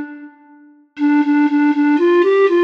There is no code.